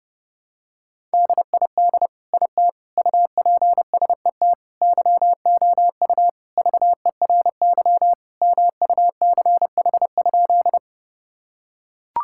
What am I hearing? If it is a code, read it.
Morse code, heard as a keyed tone, about 28 wpm, 700 Hz. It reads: DID IT UPSET YOU VERY MUCH?
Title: Sentences from Top 1000 Words - Review Part 5 - 30wpm